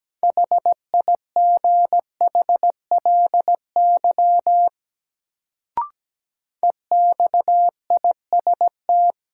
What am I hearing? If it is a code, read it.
Morse code, heard as a keyed tone, about 17 wpm, 700 Hz. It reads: HIGHLY EXIST
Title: Sentences from Top 1100 Words - New Word Drill Alt - 17wpm